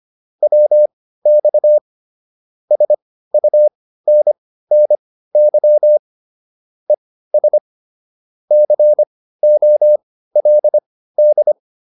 Morse code, heard as a keyed tone, about 25 wpm, spaced as for 15 wpm, 600 Hz. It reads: WX SUNNY ES COLD
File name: QSO 302 15